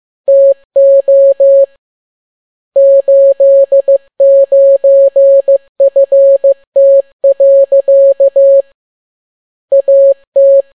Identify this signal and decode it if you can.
Morse code, heard as a keyed tone, about 15 wpm, 550 Hz. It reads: TO 89FT. AT